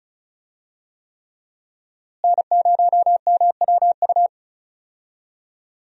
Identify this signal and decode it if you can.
Morse code, heard as a keyed tone, about 34 wpm, 700 Hz. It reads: N0MWU
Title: USA Call Signs - 5 Characters - Repeated 3x - 35wpm